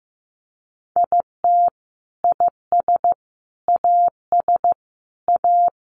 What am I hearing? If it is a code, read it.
Morse code, heard as a keyed tone, about 15 wpm, 700 Hz. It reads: IT IS AS A
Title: Sentences from Top 1000 Words 15wpm